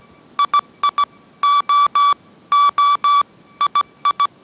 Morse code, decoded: II O O II

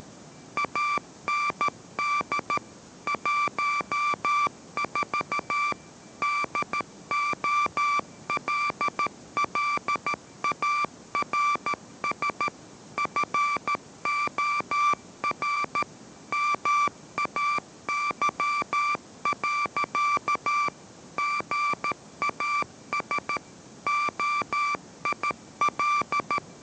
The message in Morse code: AND 14 DOLLARS FOR MAY. GAS OIL

